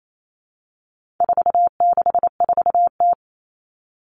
Morse code, heard as a keyed tone, about 28 wpm, 700 Hz. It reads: 464T
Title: All Cut Numbers - 4-Digit 28wpm